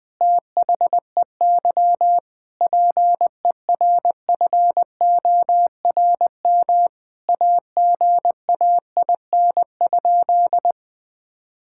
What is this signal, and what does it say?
Morse code, heard as a keyed tone, about 20 wpm, 700 Hz. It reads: THEY PERFORM AGAIN?
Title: Sentences from Top 1000 Words - Review Part 5 - 20wpm